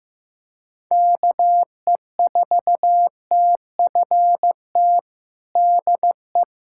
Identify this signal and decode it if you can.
Morse code, heard as a keyed tone, about 15 wpm, 700 Hz. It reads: KE4TFT DE
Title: USA Callsign DE Callsign - 6 Characters - 15wpm